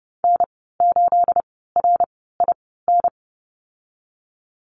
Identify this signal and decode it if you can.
Morse code, heard as a keyed tone, about 30 wpm, spaced as for 17 wpm, 700 Hz. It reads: N8RIN
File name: Lesson 37g - Letter x - Call Signs 30wpm